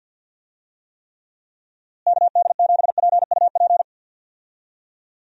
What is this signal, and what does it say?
Morse code, heard as a keyed tone, about 47 wpm, 700 Hz. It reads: KD7PUP
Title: USA Call Signs - 6 Characters - Repeated 2x - 50wpm